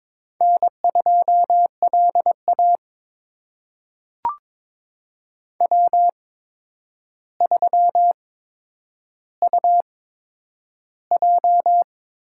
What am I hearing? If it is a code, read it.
Morse code, heard as a keyed tone, about 21 wpm, spaced as for 9 wpm, 700 Hz. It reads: N2LA W3UJ
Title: USA Call Signs - 4 Characters - 8x Character Spacing - 22wpm